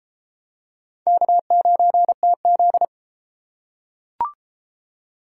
Morse code, heard as a keyed tone, about 27 wpm, 700 Hz. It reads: K9TZ
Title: Speed Racing - USA 1x2 Call Signs 22wpm